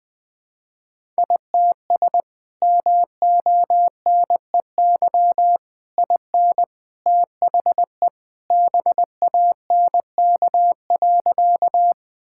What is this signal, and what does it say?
Morse code, heard as a keyed tone, about 20 wpm, 700 Hz. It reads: ITS MONEY IN THE BANK.
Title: Sentences from Top 1000 Words - Review Part 6 - 20wpm